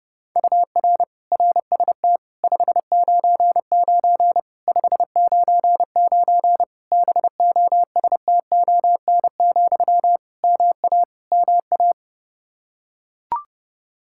Morse code, heard as a keyed tone, about 30 wpm, 700 Hz. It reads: UR RST 599 599 BOSTON, MA MA
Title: Practice QSO - Exchange 1 - Beginning - 30wpm